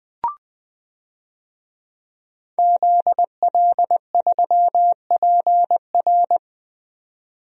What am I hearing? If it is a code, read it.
Morse code, heard as a keyed tone, about 20 wpm, 700 Hz. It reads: ZL3PR
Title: DXCC-100 Call Signs - 5 Characters - 20wpm